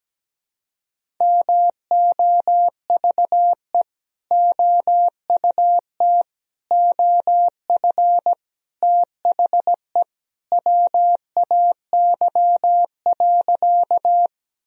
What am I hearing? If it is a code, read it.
Morse code, heard as a keyed tone, about 17 wpm, 700 Hz. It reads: MOVE OUT OF THE WAY.
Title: Sentences from Top 1000 Words - Review Part 1 - 17wpm